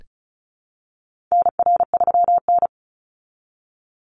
Morse code, heard as a keyed tone, about 34 wpm, 700 Hz. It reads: NR3N
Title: Call Signs Repeated 3x - 35wpm